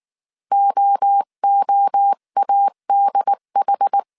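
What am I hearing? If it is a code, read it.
Morse code, heard as a keyed tone, about 19 wpm, 800 Hz. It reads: OOADH
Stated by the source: Morse code dictation at 18 wpm from the IW3SGT QRS Morse code training course, five-character groups of 300 characters.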